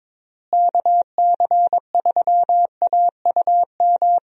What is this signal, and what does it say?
Morse code, heard as a keyed tone, about 22 wpm, 700 Hz. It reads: KC3AUM
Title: USA Call Signs - 6 Characters - Repeated 2x - 22wpm